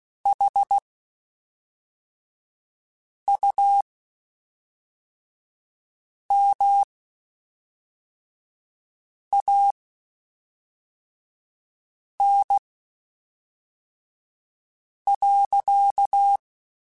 Morse code, heard as a keyed tone, about 16 wpm, spaced as for 3 wpm, 800 Hz. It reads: HUMAN.
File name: Spaced / Chuckles Spaced-mp3s / HumanErr sp @5